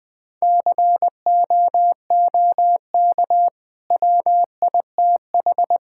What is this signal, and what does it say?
Morse code, heard as a keyed tone, about 20 wpm, 700 Hz. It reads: COOK WITH
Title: Easy Sentences with Hints - 20wpm